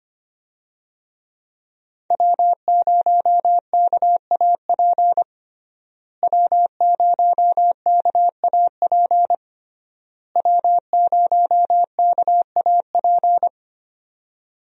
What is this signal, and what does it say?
Morse code, heard as a keyed tone, about 25 wpm, 700 Hz. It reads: W0KAP W0KAP W0KAP
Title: USA Call Signs - 5 Characters - Repeated 3x - 25wpm